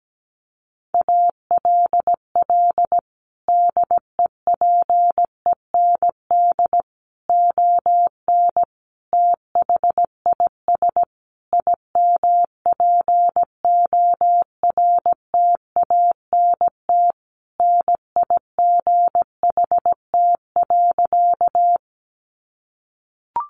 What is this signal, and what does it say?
Morse code, heard as a keyed tone, about 17 wpm, 700 Hz. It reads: ALL DEPEND ON THIS IMPORTANT NIGHT.
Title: Sentences from Top 1300 Words 17wpm